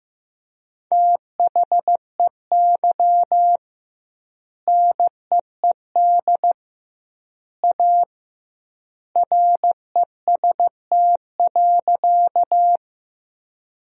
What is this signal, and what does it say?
Morse code, heard as a keyed tone, about 15 wpm, 700 Hz. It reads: THEY NEED A REST.
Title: 2x Word-Spacing - Sentences from Top 500 Words - 15wpm